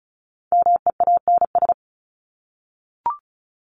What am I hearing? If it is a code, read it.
Morse code, heard as a keyed tone, about 34 wpm, 700 Hz. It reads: MEANS
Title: ES:S Suffix - Top 100 Words with Base Words 35wpm